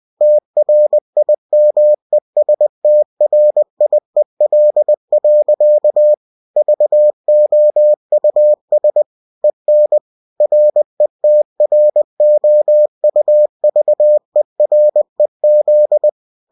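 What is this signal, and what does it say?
Morse code, heard as a keyed tone, about 20 wpm, 600 Hz. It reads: TRIMESTRIEL. VOUS EN RETROUVEREZ
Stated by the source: Bulletin sent in CW.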